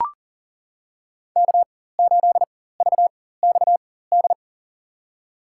Morse code, heard as a keyed tone, about 40 wpm, spaced as for 19 wpm, 700 Hz. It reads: K8VXD
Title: USA Call Signs - 5 Characters - 4x Character Spacing - 40wpm